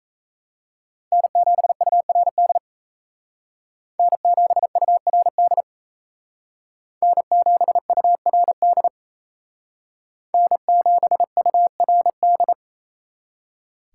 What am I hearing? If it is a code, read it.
Morse code, heard as a keyed tone, about 35 wpm, 700 Hz. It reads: N7URD N7URD N7URD N7URD
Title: Speed Racing - USA 1x3 Call Signs 28wpm